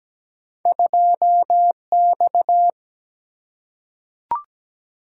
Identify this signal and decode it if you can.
Morse code, heard as a keyed tone, about 17 wpm, 700 Hz. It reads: 2X
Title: Single-Letter Single-Number 17wpm